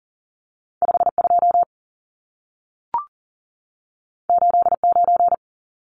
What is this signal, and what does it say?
Morse code, heard as a keyed tone, about 40 wpm, 700 Hz. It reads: 52 89
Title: Numbers - 2-Digit 40wpm